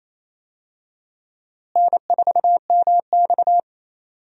Morse code, heard as a keyed tone, about 28 wpm, 700 Hz. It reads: N4MX
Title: USA Call Signs - 4 Characters - Repeated 3x - 28wpm